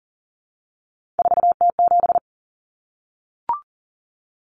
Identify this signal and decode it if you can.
Morse code, heard as a keyed tone, about 40 wpm, 700 Hz. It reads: 4T7
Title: Common Cut Numbers - 3-Digit 40wpm